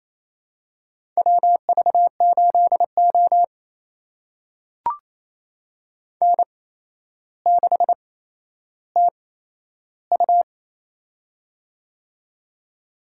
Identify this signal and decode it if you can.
Morse code, heard as a keyed tone, about 27 wpm, spaced as for 12 wpm, 700 Hz. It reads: WV8O N6TU
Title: USA Call Signs - 4 Characters - 8x Character Spacing - 28wpm